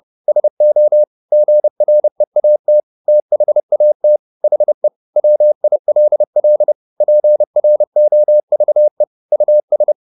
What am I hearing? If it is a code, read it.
Morse code, heard as a keyed tone, about 30 wpm, 600 Hz. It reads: SO GREAT THAT HE WILL PROVE US
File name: the old man and the sea / the old man and the sea 1 30wpm